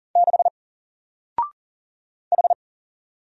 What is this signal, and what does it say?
Morse code, heard as a keyed tone, about 40 wpm, 700 Hz. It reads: 6 H
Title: Single Letter-Number 40wpm